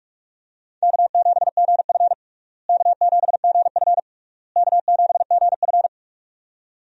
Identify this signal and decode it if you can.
Morse code, heard as a keyed tone, about 47 wpm, 700 Hz. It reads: K7GF K7GF K7GF
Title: USA Call Signs - 4 Characters - Repeated 3x - 45wpm